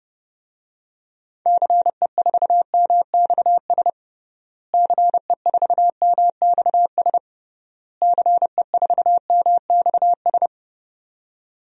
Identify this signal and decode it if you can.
Morse code, heard as a keyed tone, about 30 wpm, 700 Hz. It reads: CE4MXS CE4MXS CE4MXS
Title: DXCC-100 Call Signs - 6 Characters - Repeated 3x - 30wpm